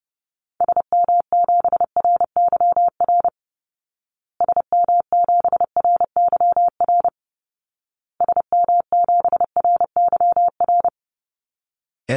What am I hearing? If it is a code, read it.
Morse code, heard as a keyed tone, about 30 wpm, 700 Hz. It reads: SM7RYR SM7RYR SM7RYR
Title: Call Signs Repeated 2x - 30wpm